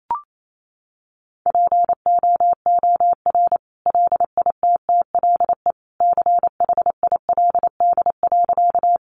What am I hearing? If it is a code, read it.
Morse code, heard as a keyed tone, about 28 wpm, 700 Hz. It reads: POOR LITTLE CHILD.
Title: Sentences from Top 1000 Words 28wpm